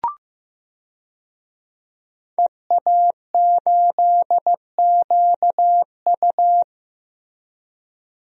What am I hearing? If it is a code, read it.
Morse code, heard as a keyed tone, about 15 wpm, 700 Hz. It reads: EA8QU
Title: DXCC-100 Call Signs - 5 Characters - Repeated 2x - 15wpm